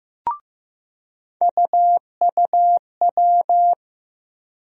Morse code, heard as a keyed tone, about 15 wpm, 700 Hz. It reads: UUW
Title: USA Call Sign Postfixes - 3 Letters 15wpm